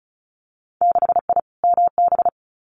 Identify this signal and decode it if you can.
Morse code, heard as a keyed tone, about 36 wpm, 700 Hz. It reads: 6I MB
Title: Winter Field Day Exchanges Encore 35wpm